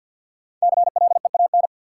Morse code, heard as a keyed tone, about 52 wpm, 700 Hz. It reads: CLEAN
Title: Sentences from Top 1100 Words - New Word Drill Alt - 50wpm